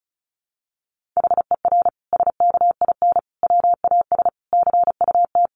Morse code, heard as a keyed tone, about 34 wpm, 700 Hz. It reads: HER SKIN WAS CUT
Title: Sentences from Top 1300 Words 35wpm